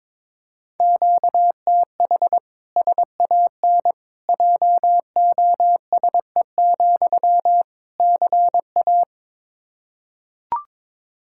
Morse code, heard as a keyed tone, about 22 wpm, 700 Hz. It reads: QTH SAN JOSE, CA
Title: Practice QSO - Exchange 1 - Beginning - 22wpm